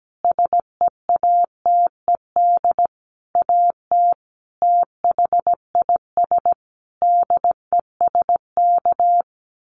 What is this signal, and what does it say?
Morse code, heard as a keyed tone, about 17 wpm, 700 Hz. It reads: SEATED AT THIS DESK
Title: A Sound Of Thunder 17wpm